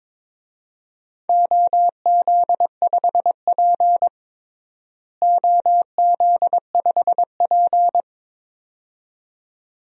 Morse code, heard as a keyed tone, about 22 wpm, 700 Hz. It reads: OZ5P OZ5P
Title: DXCC-100 Call Signs - 4 Characters - Repeated 2x - 22wpm